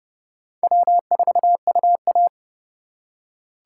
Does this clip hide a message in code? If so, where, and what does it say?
Yes: Morse code from 0.6 to 2.3 s, heard as a keyed tone, about 29 wpm, 700 Hz. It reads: W4UA